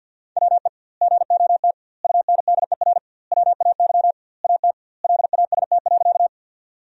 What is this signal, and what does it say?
Morse code, heard as a keyed tone, about 50 wpm, 700 Hz. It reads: WE GOT UNDER WAY AT LAST.
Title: 2x Word-Spacing - Sentences from Top 300 Words - 50wpm